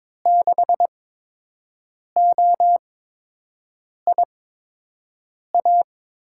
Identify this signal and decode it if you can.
Morse code, heard as a keyed tone, about 22 wpm, spaced as for 6 wpm, 700 Hz. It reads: 6OIA
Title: USA Call Signs - 8x Character Spacing - 22wpm